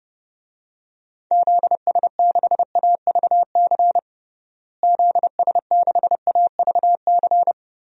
Morse code, heard as a keyed tone, about 32 wpm, 700 Hz. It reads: ZS6AVC ZS6AVC
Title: DXCC-100 Call Signs - 6 Characters - Repeated 3x - 30wpm